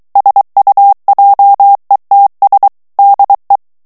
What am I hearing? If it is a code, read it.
Morse code, heard as a keyed tone, about 23 wpm, 800 Hz. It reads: SUJETS DE
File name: FRANCAIS TEXTE CLAIR 2 VITESSE 22 MOTS MINUTES